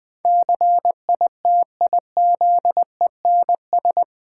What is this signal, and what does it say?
Morse code, heard as a keyed tone, about 20 wpm, 700 Hz. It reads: CITIZENS
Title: Build A Word - 4 Letters At A Time - 11-Letter Words - 20wpm